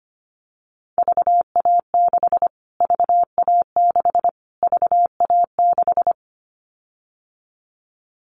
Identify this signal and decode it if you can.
Morse code, heard as a keyed tone, about 25 wpm, 700 Hz. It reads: VA6 VA6 VA6
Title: Callsign Prefixes for Top 25 DXCC - Repeated 3x - 25wpm